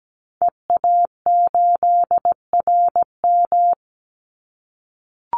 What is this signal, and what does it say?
Morse code, heard as a keyed tone, about 17 wpm, 700 Hz. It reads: EA8RM
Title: Call Signs - 17wpm